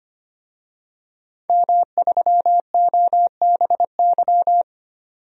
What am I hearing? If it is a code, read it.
Morse code, heard as a keyed tone, about 25 wpm, 700 Hz. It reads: M3OBY
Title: DXCC-100 Call Signs - 5 Characters - 25wpm